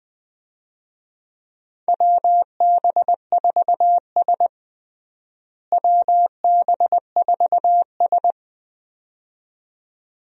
Morse code, heard as a keyed tone, about 20 wpm, 700 Hz. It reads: WB4S WB4S
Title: USA Call Signs - 4 Characters - Repeated 2x - 20wpm